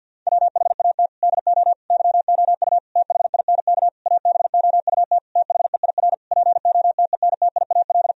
Morse code, heard as a keyed tone, about 50 wpm, 700 Hz. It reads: WHAT DO YOU THINK ABOUT THEIR POTENTIAL
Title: Sentences from Top 1000 Words - Review Part 4 - 50wpm